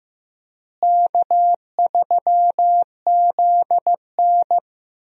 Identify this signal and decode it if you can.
Morse code, heard as a keyed tone, about 15 wpm, 700 Hz. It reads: K3ZN
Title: USA Call Signs - 4 Characters - Repeated 2x - 15wpm